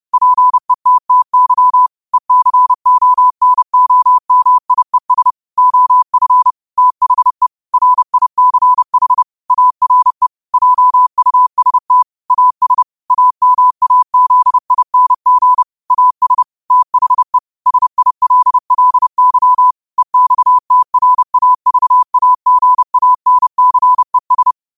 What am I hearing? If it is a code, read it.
Morse code, heard as a keyed tone, about 30 wpm, 1000 Hz. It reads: PETTY ECONOMIES OF THE RICH ARE JUST AS AMAZING AS THE SILLY EXTRAVAGANCES